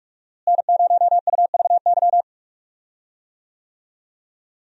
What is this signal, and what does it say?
Morse code, heard as a keyed tone, about 47 wpm, 700 Hz. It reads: N0UVY
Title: USA Call Signs - 5 Characters - Repeated 2x - 45wpm